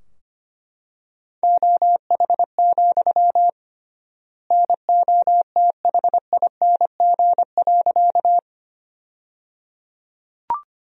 Morse code, heard as a keyed tone, about 25 wpm, 700 Hz. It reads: OH, NOTHING.
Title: 3x Word-Spacing - Sentences from Top 500 Words - 25wpm